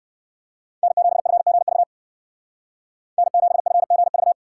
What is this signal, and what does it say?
Morse code, heard as a keyed tone, about 58 wpm, 700 Hz. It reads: N7FGV N7FGV